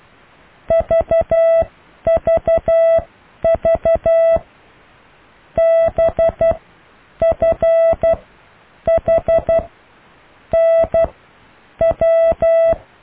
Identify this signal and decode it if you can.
Morse code, heard as a keyed tone, about 12 wpm, spaced as for 8 wpm, 660 Hz. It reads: VVV BFHNW